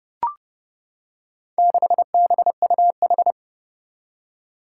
Morse code, heard as a keyed tone, about 30 wpm, 700 Hz. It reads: 6BUH
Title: Four Challenging Letter-Number Combinations 30wpm